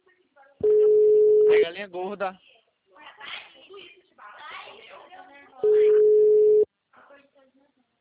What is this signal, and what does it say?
Ringback tone (Europe, 425 Hz), with speech.